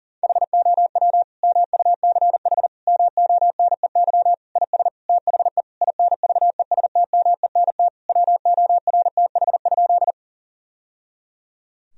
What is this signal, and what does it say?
Morse code, heard as a keyed tone, about 40 wpm, 700 Hz. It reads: HOW MUCH MONEY IS THE INVESTMENT WORTH?
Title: Sentences from Top 1000 Words - Review Part 4 - 40wpm